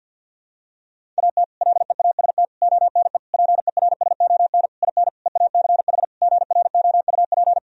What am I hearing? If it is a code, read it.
Morse code, heard as a keyed tone, about 51 wpm, 700 Hz. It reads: AT LEAST ONE PERSON IN EACH GROUP
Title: Sentences from Top 1000 Words - Review Part 1 - 50wpm